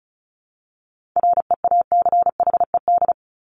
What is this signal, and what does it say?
Morse code, heard as a keyed tone, about 34 wpm, 700 Hz. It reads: REACHED
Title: ED Suffix - Top 100 Words with Base Words 35wpm